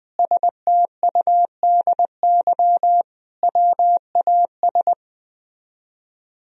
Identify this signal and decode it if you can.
Morse code, heard as a keyed tone, about 20 wpm, 700 Hz. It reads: STUDY WAS
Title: Speed-Racing - Sets of 5 Words 17wpm